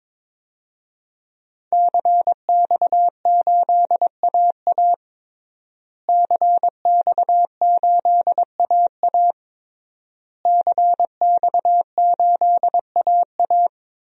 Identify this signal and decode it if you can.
Morse code, heard as a keyed tone, about 22 wpm, 700 Hz. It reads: CX8AA CX8AA CX8AA
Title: DXCC-100 Call Signs - 5 Characters - Repeated 3x - 22wpm